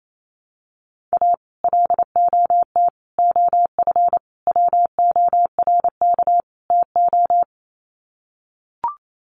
Morse code, heard as a keyed tone, about 28 wpm, 700 Hz. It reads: A LOT OF WORK TO
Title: Sets of 5 Words Encore2 28wpm